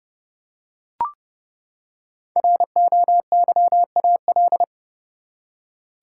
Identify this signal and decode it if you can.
Morse code, heard as a keyed tone, about 30 wpm, 700 Hz. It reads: ROYAL
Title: Five Letter Words - 30wpm